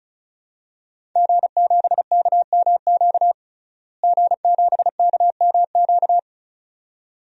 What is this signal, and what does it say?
Morse code, heard as a keyed tone, about 34 wpm, 700 Hz. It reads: G7KMQ G7KMQ
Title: DXCC-100 Call Signs - 5 Characters - Repeated 2x - 35wpm